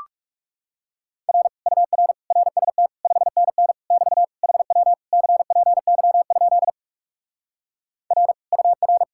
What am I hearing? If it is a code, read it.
Morse code, heard as a keyed tone, about 40 wpm, 700 Hz. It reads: R UR RST 5NN <BT> HW CPY? R UR